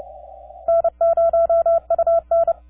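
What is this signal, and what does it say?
Morse code, heard as a keyed tone, about 29 wpm, 670 Hz. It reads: N0UN
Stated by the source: 3Y0K on Bouvet Island, DX on 10 meters CW, heard by N0UN in Denver during a weak opening that faded out.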